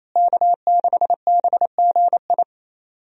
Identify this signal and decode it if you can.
Morse code, heard as a keyed tone, about 28 wpm, 700 Hz. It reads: K6BGI
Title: USA Call Signs - 5 Characters - Repeated 2x - 28wpm